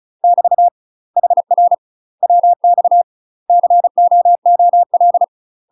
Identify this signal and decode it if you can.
Morse code, heard as a keyed tone, about 36 wpm, 700 Hz. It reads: <BT> HR WX COOL
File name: UK MASTER QSOs TRAINING / 35WPM / UK MASTER QSOs 0014